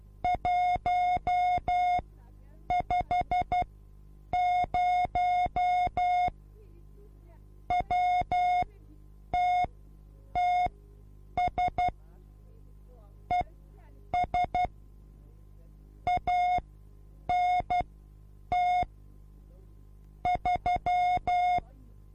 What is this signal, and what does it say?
Morse code, heard as a keyed tone, about 12 wpm, spaced as for 8 wpm, 720 Hz. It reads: 150 WTTS ES ANT 3
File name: repos / Morsekurs / DJ4UF Morsen24